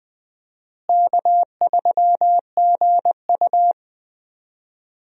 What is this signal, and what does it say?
Morse code, heard as a keyed tone, about 20 wpm, 700 Hz. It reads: K3GU
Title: USA Call Signs - 4 Characters - Repeated 3x - 20wpm